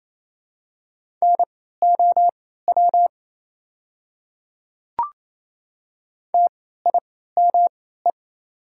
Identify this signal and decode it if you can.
Morse code, heard as a keyed tone, about 28 wpm, spaced as for 16 wpm, 700 Hz. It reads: NOW TIME
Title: 3x Character Spacing - Top 100 Words - 28wpm